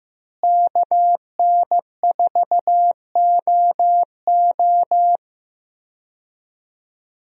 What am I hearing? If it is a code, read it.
Morse code, heard as a keyed tone, about 15 wpm, 700 Hz. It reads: KN4OO
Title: USA Call Signs - 5 Characters - Repeated 3x - 15wpm